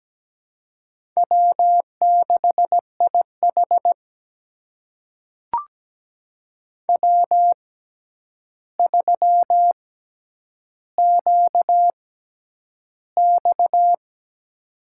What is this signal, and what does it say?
Morse code, heard as a keyed tone, about 17 wpm, spaced as for 9 wpm, 700 Hz. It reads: W6IH W3QX
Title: USA Call Signs - 4 Characters - 6x Character Spacing - 17wpm